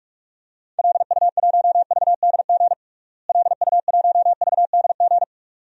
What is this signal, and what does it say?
Morse code, heard as a keyed tone, about 42 wpm, 700 Hz. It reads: LU1VDG LU1VDG
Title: DXCC-100 Call Signs - 6 Characters - Repeated 2x - 45wpm